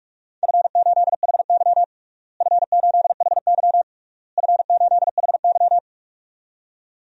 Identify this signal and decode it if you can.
Morse code, heard as a keyed tone, about 46 wpm, 700 Hz. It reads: F8HY F8HY F8HY
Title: DXCC-100 Call Signs - 4 Characters - Repeated 3x - 45wpm